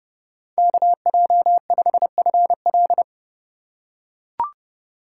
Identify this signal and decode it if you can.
Morse code, heard as a keyed tone, about 30 wpm, 700 Hz. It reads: KJ5FL